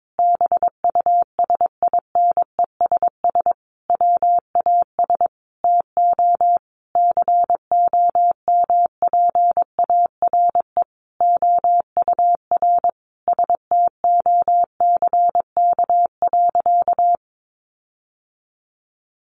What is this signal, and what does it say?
Morse code, heard as a keyed tone, about 22 wpm, 700 Hz. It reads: BUSINESS WAS TO COMPARE OUR STOCK.
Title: Sentences from Top 1300 Words 22wpm